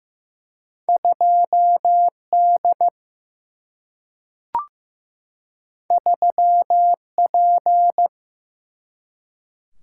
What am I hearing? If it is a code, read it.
Morse code, heard as a keyed tone, about 15 wpm, 700 Hz. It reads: 2D 3P
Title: Single-Letter Single-Number 15wpm